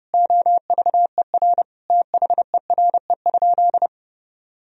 Morse code, heard as a keyed tone, about 32 wpm, 700 Hz. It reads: OVER THERE?
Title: Sentences from Top 1000 Words - Review Part 2 - 30wpm